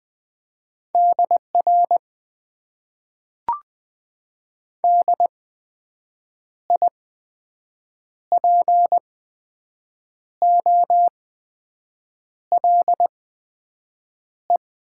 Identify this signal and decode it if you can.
Morse code, heard as a keyed tone, about 21 wpm, spaced as for 6 wpm, 700 Hz. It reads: DR DIPOLE
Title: Top 90 QSO Elements - 8x Character Spacing - 20wpm